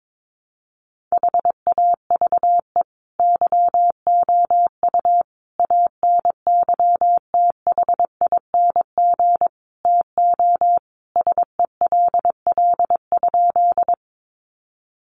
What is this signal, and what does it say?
Morse code, heard as a keyed tone, about 22 wpm, 700 Hz. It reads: HAVE YOU ANYTHING TO SELL?
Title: Sentences from Top 1100 Words 22wpm